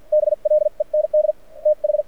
Morse code, heard as a keyed tone, about 50 wpm, spaced as for 41 wpm, 600 Hz. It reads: BLEND TH
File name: archived / CRSnet